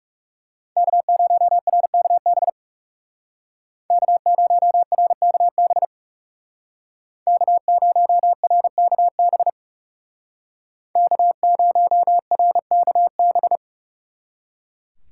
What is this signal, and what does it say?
Morse code, heard as a keyed tone, about 38 wpm, 700 Hz. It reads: K0RKB K0RKB K0RKB K0RKB